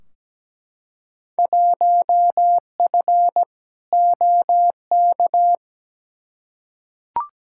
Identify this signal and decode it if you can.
Morse code, heard as a keyed tone, about 17 wpm, 700 Hz. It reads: 1F OK
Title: Field Day Exchanges Encore - 17wpm